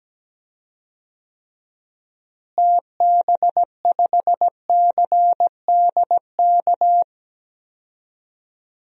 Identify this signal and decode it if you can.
Morse code, heard as a keyed tone, about 17 wpm, 700 Hz. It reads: TB5CDK
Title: DXCC-100 Call Signs - 6 Characters - Repeated 2x - 17wpm